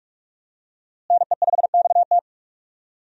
Morse code, heard as a keyed tone, about 46 wpm, 700 Hz. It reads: NE5XT
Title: USA Call Signs - 5 Characters - Repeated 3x - 45wpm